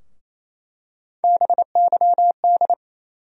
Morse code, heard as a keyed tone, about 28 wpm, 700 Hz. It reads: BYD
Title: USA Call Sign Postfixes - 3 Letters 28wpm